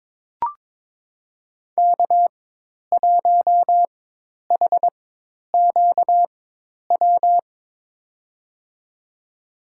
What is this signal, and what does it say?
Morse code, heard as a keyed tone, about 23 wpm, spaced as for 10 wpm, 700 Hz. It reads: K1HQW
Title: USA Call Signs - 4x Character Spacing - 22wpm